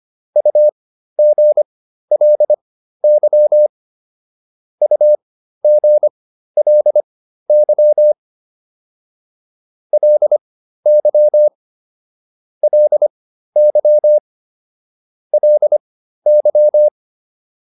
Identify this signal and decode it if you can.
Morse code, heard as a keyed tone, about 25 wpm, spaced as for 13 wpm, 600 Hz. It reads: UGLY UGLY LY LY LY